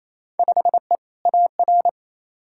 Morse code, heard as a keyed tone, about 27 wpm, 700 Hz. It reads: 5E AR
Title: Field Day Exchanges Encore - 28wpm